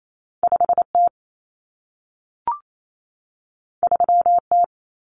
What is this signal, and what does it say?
Morse code, heard as a keyed tone, about 28 wpm, 700 Hz. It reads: 5T 3T
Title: Common Cut Numbers - 2-Digit 28wpm